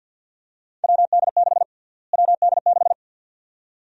Morse code, heard as a keyed tone, about 52 wpm, 700 Hz. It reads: WD6 WD6
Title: USA Call Sign Prefixes - Repeated 2x - 50wpm